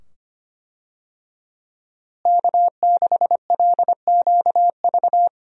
Morse code, heard as a keyed tone, about 25 wpm, 700 Hz. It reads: K6LQV
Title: USA Call Signs - 5 Characters - Repeated 2x - 25wpm